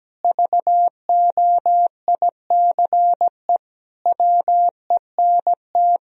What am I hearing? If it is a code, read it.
Morse code, heard as a keyed tone, about 17 wpm, 700 Hz. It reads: VOICE WENT
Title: Sentences from Top 1000 Words - Review Part 1 - 17wpm